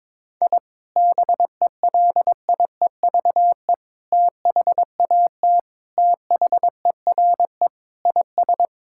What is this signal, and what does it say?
Morse code, heard as a keyed tone, about 22 wpm, 700 Hz. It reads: I BELIEVE THAT THERE IS
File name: practice / 22wpm / Sentences with 3-Word Phrases - 22wpm